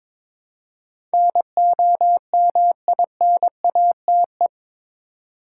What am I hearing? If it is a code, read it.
Morse code, heard as a keyed tone, about 22 wpm, 700 Hz. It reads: NOMINATE
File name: practice / 22wpm / Eight Letter Words - 22wpm